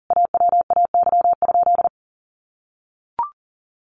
Morse code, heard as a keyed tone, about 40 wpm, 700 Hz. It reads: AWAY?
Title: Sentences from Top 600 Words 40wpm